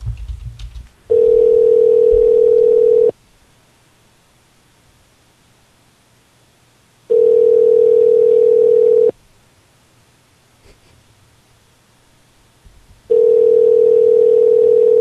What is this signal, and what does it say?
Ringback tone (North America), 440 + 480 Hz.